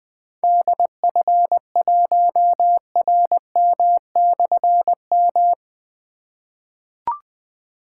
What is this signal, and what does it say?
Morse code, heard as a keyed tone, about 20 wpm, 700 Hz. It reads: DF1RM/M
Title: Difficult Call Signs 20wpm